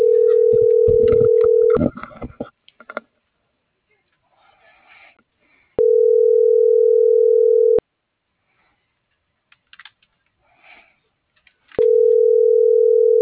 Ringback tone (North America), 440 + 480 Hz.
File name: CENTRAL MAYAB / out lizbeth.mayab - .25634